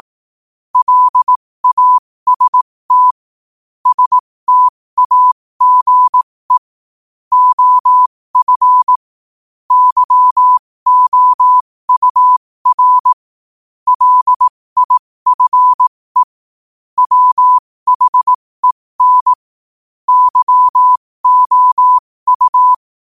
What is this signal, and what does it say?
Morse code, heard as a keyed tone, about 18 wpm, spaced as for 15 wpm, 1000 Hz. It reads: LAST STAGE OF YOUR LIFE WHEN YOU
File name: qotdfiles / 15wpm